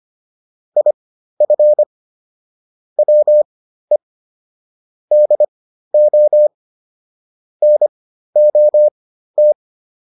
Morse code, heard as a keyed tone, about 25 wpm, spaced as for 13 wpm, 600 Hz. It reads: IF WE DO NOT